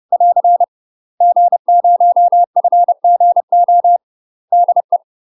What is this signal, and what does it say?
Morse code, heard as a keyed tone, about 30 wpm, 700 Hz. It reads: <AR> G0FGO DE
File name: UK MASTER QSOs TRAINING / 30WPM / UK MASTER QSOs 0015